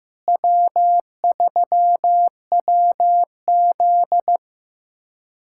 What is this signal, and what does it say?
Morse code, heard as a keyed tone, about 15 wpm, 700 Hz. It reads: W3WZ
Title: USA Callsign DE Callsign 15wpm